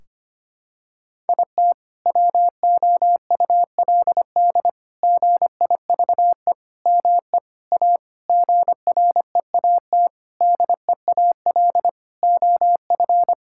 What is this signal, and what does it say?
Morse code, heard as a keyed tone, about 25 wpm, 700 Hz. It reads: IT WOULD GIVE ME A GREAT DEAL OF
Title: Sentences from Top 1600 Words 25wpm